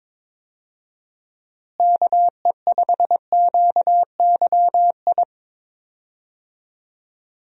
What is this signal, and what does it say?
Morse code, heard as a keyed tone, about 22 wpm, 700 Hz. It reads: KE5QYI
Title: USA Call Signs - 6 Characters - Repeated 2x - 22wpm